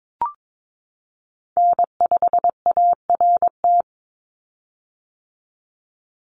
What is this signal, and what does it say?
Morse code, heard as a keyed tone, about 22 wpm, 700 Hz. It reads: N5ART
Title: USA Call Signs 22wpm